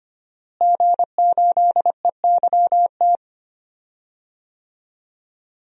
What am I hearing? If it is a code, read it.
Morse code, heard as a keyed tone, about 25 wpm, 700 Hz. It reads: G8EYT
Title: DXCC-100 Call Signs - 5 Characters - Repeated 3x - 25wpm